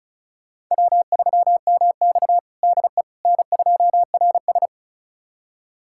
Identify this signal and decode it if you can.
Morse code, heard as a keyed tone, about 36 wpm, 700 Hz. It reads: W3MX DE N2RS
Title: USA Callsign DE Callsign - 4 Characters - 35wpm